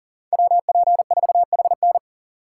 Morse code, heard as a keyed tone, about 37 wpm, 700 Hz. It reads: WP4HN
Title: USA Call Signs - 5 Characters - 6x Character Spacing - 40wpm